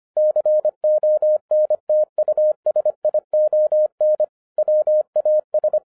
Morse code, heard as a keyed tone, about 25 wpm, 610 Hz. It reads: CONTUSION WAS
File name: 2011-07-mp3 / n59 BACK FROM A BIG ACCIDENT 25wpm